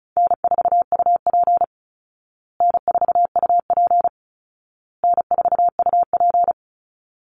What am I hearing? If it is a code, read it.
Morse code, heard as a keyed tone, about 34 wpm, 700 Hz. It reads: N4UP N4UP N4UP
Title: Easy Call Signs Repeated 3x - 35wpm